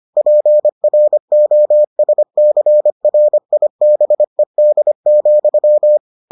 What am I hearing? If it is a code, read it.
Morse code, heard as a keyed tone, about 25 wpm, 600 Hz. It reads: PROSCRIBED,